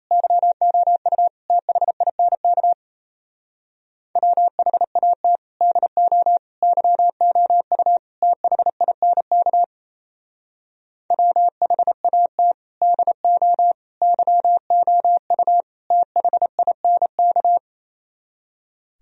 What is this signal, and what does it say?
Morse code, heard as a keyed tone, about 38 wpm, 700 Hz. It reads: YOU THINK WHAT DO YOU THINK WHAT DO YOU THINK